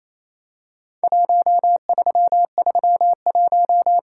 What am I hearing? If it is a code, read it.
Morse code, heard as a keyed tone, about 28 wpm, 700 Hz. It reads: 1331
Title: Worked All Europe DX Contest - QTC Exchange - 28wpm